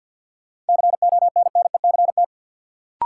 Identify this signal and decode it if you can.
Morse code, heard as a keyed tone, about 50 wpm, 700 Hz. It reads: CONNECT
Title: Sentences from Top 1500 Words - New Word Drill - 50wpm